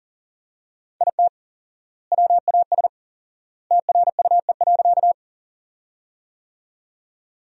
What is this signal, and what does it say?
Morse code, heard as a keyed tone, about 40 wpm, 700 Hz. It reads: IT WAS TRUE.
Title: 4x Word-Spacing - Sentences from Top 400 Words - 40wpm